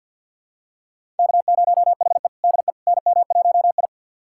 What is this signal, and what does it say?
Morse code, heard as a keyed tone, about 52 wpm, 700 Hz. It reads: K0HE DE NM1I
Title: USA Callsign DE Callsign - 4 Characters - 50wpm